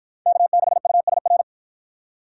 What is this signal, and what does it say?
Morse code, heard as a keyed tone, about 53 wpm, 700 Hz. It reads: K6USR